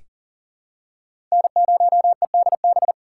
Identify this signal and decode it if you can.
Morse code, heard as a keyed tone, about 40 wpm, 700 Hz. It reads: N0EDB
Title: USA Call Signs - 5 Characters - 6x Character Spacing - 40wpm